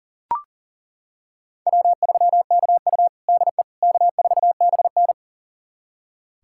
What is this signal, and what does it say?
Morse code, heard as a keyed tone, about 40 wpm, 700 Hz. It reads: W3KU DE K4BN
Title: USA Callsign DE Callsign - 4 Characters - 40wpm